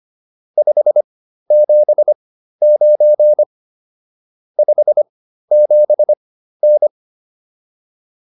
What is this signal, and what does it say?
Morse code, heard as a keyed tone, about 25 wpm, spaced as for 13 wpm, 600 Hz. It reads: 579 57N